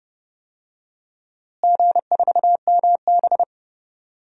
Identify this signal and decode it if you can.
Morse code, heard as a keyed tone, about 30 wpm, 700 Hz. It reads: G4MB
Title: DXCC-100 Call Signs - 4 Characters - Repeated 3x - 30wpm